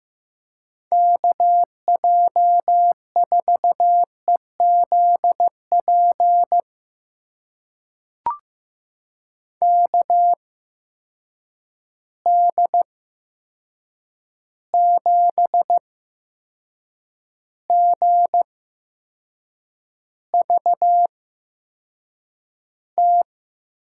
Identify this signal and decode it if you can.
Morse code, heard as a keyed tone, about 15 wpm, spaced as for 6 wpm, 700 Hz. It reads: KJ4EZP KD7GVT
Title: USA Call Signs - 6 Characters - 8x Character Spacing - 15wpm